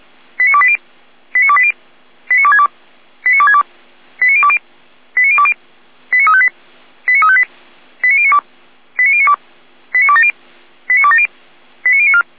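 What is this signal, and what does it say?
Five-tone selective call (ZVEI), heard as quick runs of five tones, tones of about 70 ms. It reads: 89280 89280 89262 89262 89020 89020 89358 89358 89092 89092 89270 89270 89094
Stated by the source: ZVEI five-tone selective call 89230, received by SDR.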